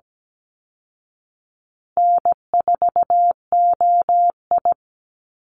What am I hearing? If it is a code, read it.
Morse code, heard as a keyed tone, about 17 wpm, 700 Hz. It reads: N4OI
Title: Easy Call Signs Repeated 3x - 17wpm